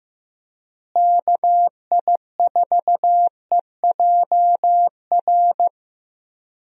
Morse code, heard as a keyed tone, about 15 wpm, 700 Hz. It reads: KI4EJR
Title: USA Call Signs - 6 Characters - Repeated 3x - 15wpm